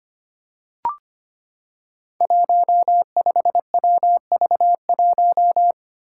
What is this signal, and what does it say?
Morse code, heard as a keyed tone, about 25 wpm, 700 Hz. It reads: 15WV1
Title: Five Challenging Letter-Number Combinations 25wpm